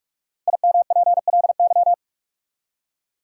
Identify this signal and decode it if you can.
Morse code, heard as a keyed tone, about 44 wpm, 700 Hz. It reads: IMPLY
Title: Five Letter Words - 45wpm